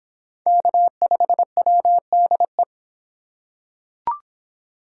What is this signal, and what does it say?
Morse code, heard as a keyed tone, about 24 wpm, 700 Hz. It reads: K5WDE